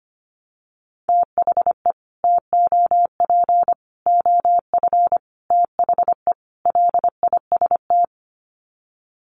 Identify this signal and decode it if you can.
Morse code, heard as a keyed tone, about 25 wpm, 700 Hz. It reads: THE TOP OF THE LIST
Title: Sets of 5 Words Encore2 25wpm